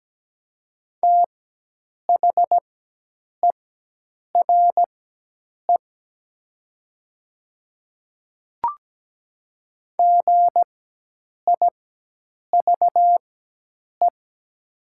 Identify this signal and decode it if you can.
Morse code, heard as a keyed tone, about 17 wpm, spaced as for 8 wpm, 700 Hz. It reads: THERE GIVE
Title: 4x Character Spacing - Top 200 Words - 17wpm